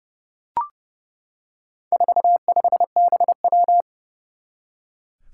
Morse code, heard as a keyed tone, about 30 wpm, 690 Hz. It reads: 45BW
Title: Four Challenging Letter-Number Combinations 30wpm